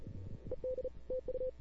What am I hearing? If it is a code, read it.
Morse code, heard as a keyed tone, about 37 wpm, 500 Hz. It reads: ED TU
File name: CW / 2005-k5tr arrl cw dx -00:00